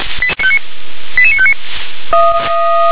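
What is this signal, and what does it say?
Five-tone selective call (ZVEI), heard as quick runs of five tones, tones of about 70 ms. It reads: 88959
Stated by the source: ZVEI five-tone selective call received by SDR.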